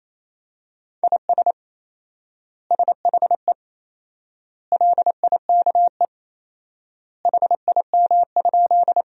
Morse code, heard as a keyed tone, about 28 wpm, 700 Hz. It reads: IS SHE LIKE HIM?
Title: 4x Word-Spacing - Sentences from Top 100 Words - 28wpm